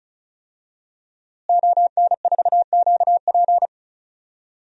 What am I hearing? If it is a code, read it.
Morse code, heard as a keyed tone, about 36 wpm, 700 Hz. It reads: ON4QP
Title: DXCC-100 Call Signs - 5 Characters - Repeated 2x - 35wpm